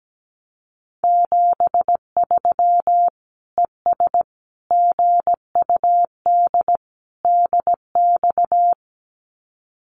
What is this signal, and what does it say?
Morse code, heard as a keyed tone, about 17 wpm, 700 Hz. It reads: 73 ES GUD DX
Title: Common QSO Phrases 17wpm